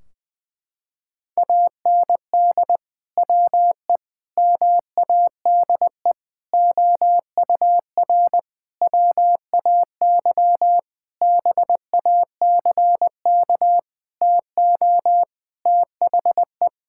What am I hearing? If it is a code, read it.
Morse code, heard as a keyed tone, about 21 wpm, 700 Hz. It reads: AND WE MADE OUR WAY BACK TO THE
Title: Sentences from Top 1600 Words 20wpm